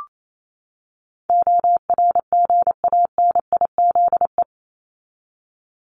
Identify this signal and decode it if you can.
Morse code, heard as a keyed tone, about 28 wpm, 700 Hz. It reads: ORGANIZE
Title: Top 1751-2000 Words 28wpm